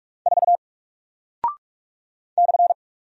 Morse code, heard as a keyed tone, about 44 wpm, 700 Hz. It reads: 4 /
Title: Single Character (Letter:Number:Punctuation:Prosign) Alt 45wpm